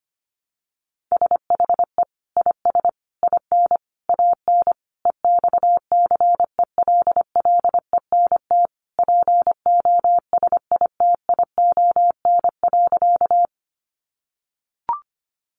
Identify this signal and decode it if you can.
Morse code, heard as a keyed tone, about 25 wpm, 700 Hz. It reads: SHE IS IN AN EXCELLENT POSITION.
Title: Sentences from Top 1000 Words 25wpm